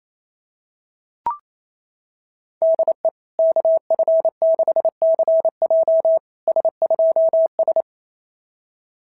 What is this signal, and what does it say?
Morse code, heard as a keyed tone, about 27 wpm, 660 Hz. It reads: DE KF6CJ S2S
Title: SOTA QSO Practice 28wpm